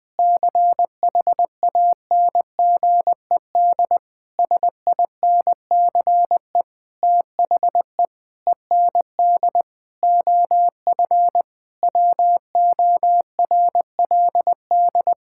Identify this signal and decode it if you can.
Morse code, heard as a keyed tone, about 20 wpm, 700 Hz. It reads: CHANGED SINCE THE END OF WORLD
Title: Sentences with 7-Word Phrases - 20wpm